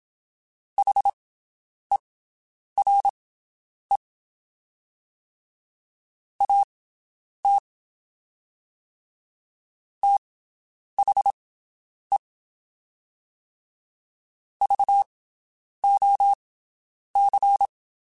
Morse code, heard as a keyed tone, about 27 wpm, spaced as for 9 wpm, 800 Hz. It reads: HERE AT THE VOC
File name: (spaced) / QSOs Spaced-mp3s / WD0BGP 1 @25 sp